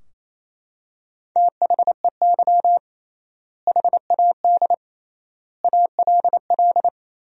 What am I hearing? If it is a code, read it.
Morse code, heard as a keyed tone, about 27 wpm, 700 Hz. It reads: THEY HAD ALL
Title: 3x Word-Spacing - Sentences from Top 200 Words - 28wpm